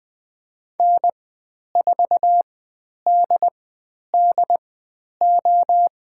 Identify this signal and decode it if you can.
Morse code, heard as a keyed tone, about 20 wpm, spaced as for 10 wpm, 700 Hz. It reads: N4DDO
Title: 095 Lesson 16g - Letter c - Call Signs 20wpm